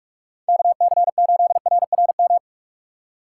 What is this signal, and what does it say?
Morse code, heard as a keyed tone, about 46 wpm, 700 Hz. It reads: KC8RRM
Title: USA Call Signs - 2x Character Spacing - 45wpm